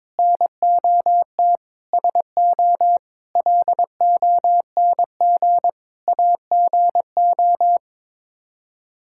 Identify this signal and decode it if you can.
Morse code, heard as a keyed tone, about 22 wpm, 700 Hz. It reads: NOT SO LONG AGO
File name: practice / 22wpm / Sets of 8 Words - 22wpm